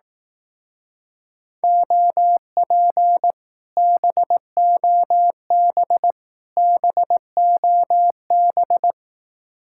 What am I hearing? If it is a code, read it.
Morse code, heard as a keyed tone, about 18 wpm, 700 Hz. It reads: OP BOB BOB
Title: Speed-Racing Common QSO Phrases Encore - 15wpm